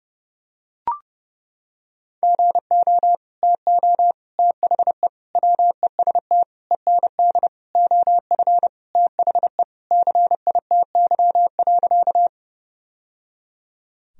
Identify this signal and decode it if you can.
Morse code, heard as a keyed tone, about 30 wpm, 700 Hz. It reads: GO TO THE WEST END OF THE CITY.
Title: Sentences from Top 1000 Words - Review Part 2 - 30wpm